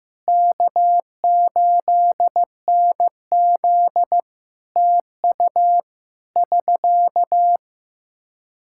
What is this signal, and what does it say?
Morse code, heard as a keyed tone, about 15 wpm, 700 Hz. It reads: K8NZ TU <SK>
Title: Practice QSO - Exchange 4 - Ending - 15wpm